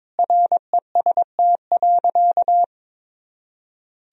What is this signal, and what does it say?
Morse code, heard as a keyed tone, about 22 wpm, 700 Hz. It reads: REST.
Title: Sentences from Top 500 Words and Abbreviations 22wpm